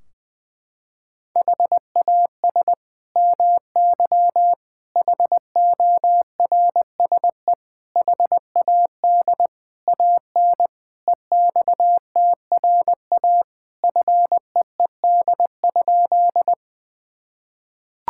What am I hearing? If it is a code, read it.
Morse code, heard as a keyed tone, about 20 wpm, 700 Hz. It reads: HAS MY HORSE HAD AN EXTRA FEED?
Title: Sentences from Top 1600 Words 20wpm